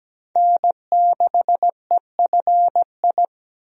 Morse code, heard as a keyed tone, about 17 wpm, 700 Hz. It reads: N6EFI
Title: USA Call Signs - 5 Characters - 17wpm